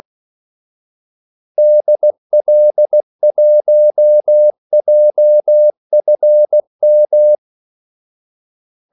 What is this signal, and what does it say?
Morse code, heard as a keyed tone, about 16 wpm, 600 Hz. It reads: DL1JFM
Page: Kaldesignaler 16 wpm